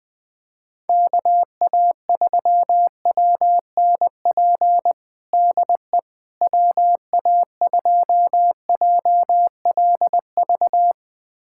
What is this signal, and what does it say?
Morse code, heard as a keyed tone, about 20 wpm, 700 Hz. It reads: KA3WNP DE WA2JLV
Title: USA Callsign DE Callsign - 6 Characters - 20wpm